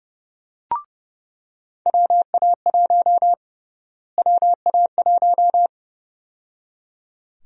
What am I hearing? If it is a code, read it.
Morse code, heard as a keyed tone, about 30 wpm, 700 Hz. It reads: WA1 WA1
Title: USA Call Sign Prefixes - Repeated 2x - 30wpm